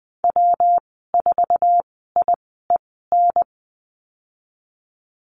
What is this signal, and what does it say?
Morse code, heard as a keyed tone, about 20 wpm, spaced as for 15 wpm, 700 Hz. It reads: W4IEN
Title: 179 Lesson 29g - Slash - Call Signs 20wpm